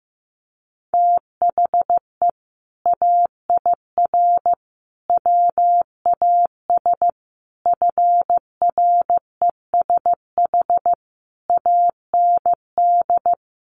Morse code, heard as a keyed tone, about 15 wpm, 700 Hz. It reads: THE AIR WAS FRESH AND
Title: Sentences from Top 1200 Words 15wpm